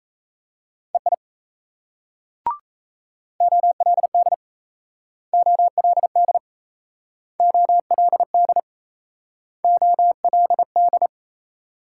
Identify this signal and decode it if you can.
Morse code, heard as a keyed tone, about 37 wpm, 700 Hz. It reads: EI OLD OLD OLD OLD